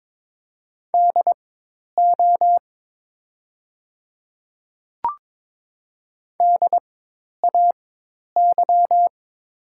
Morse code, heard as a keyed tone, about 22 wpm, spaced as for 10 wpm, 700 Hz. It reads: DO DAY
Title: 4x Character Spacing - Top 200 Words - 22wpm